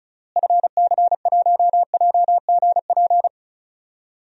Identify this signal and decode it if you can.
Morse code, heard as a keyed tone, about 36 wpm, 700 Hz. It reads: FC1JGP
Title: DXCC-100 Call Signs - 6 Characters - Repeated 3x - 35wpm